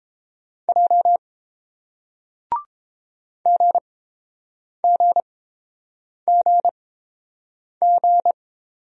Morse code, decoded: J GGGG

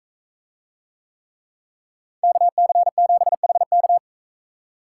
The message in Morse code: KC7HK

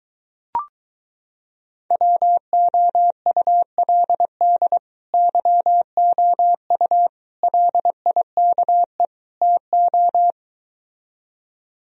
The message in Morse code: WOULD YOU LIKE TO

